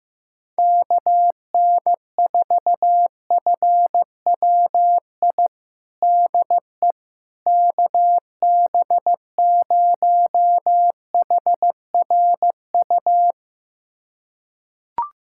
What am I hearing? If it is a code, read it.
Morse code, heard as a keyed tone, about 15 wpm, 700 Hz. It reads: KN4FWI DE KB0HRU
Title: USA Callsign DE Callsign - 6 Characters - 15wpm